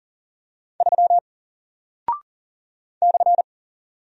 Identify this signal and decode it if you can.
Morse code, heard as a keyed tone, about 40 wpm, 700 Hz. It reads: 3 /
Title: Single Character (Letter:Number:Punctuation:Prosign) Alt 40wpm